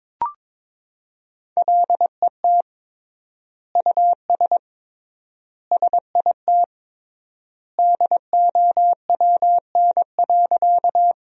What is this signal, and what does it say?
Morse code, heard as a keyed tone, about 22 wpm, 700 Hz. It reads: LET US SIT DOWN.
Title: 3x Word-Spacing - Sentences from Top 500 Words - 22wpm